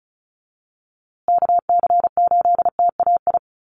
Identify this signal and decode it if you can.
Morse code, heard as a keyed tone, about 34 wpm, 700 Hz. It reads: KC8TAI
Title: USA Call Signs Repeated 3x 35wpm